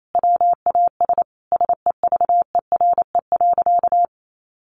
Morse code, heard as a keyed tone, about 28 wpm, 700 Hz. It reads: WAS SEVERE.
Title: Sentences from Top 1400 Words 28wpm